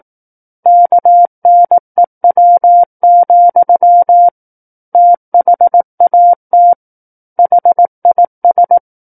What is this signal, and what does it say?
Morse code, heard as a keyed tone, about 18 wpm, 700 Hz. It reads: KNEW, THAT HIS
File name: 18 WPM